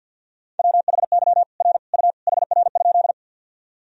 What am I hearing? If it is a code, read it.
Morse code, heard as a keyed tone, about 50 wpm, 700 Hz. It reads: WHY R U HR?
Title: Sentences from Top 200 Words and Abbreviations 50wpm